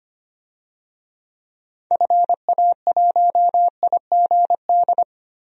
Morse code, heard as a keyed tone, about 25 wpm, 700 Hz. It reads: FA1IGD
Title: DXCC-100 Call Signs - 6 Characters - 25wpm